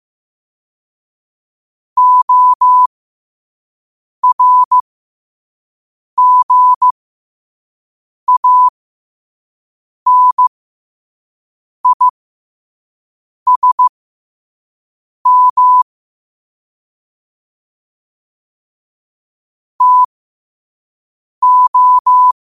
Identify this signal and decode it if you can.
Morse code, heard as a keyed tone, about 15 wpm, spaced as for 5 wpm, 1000 Hz. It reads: ORGANISM TO
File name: qotdfiles / 5wpm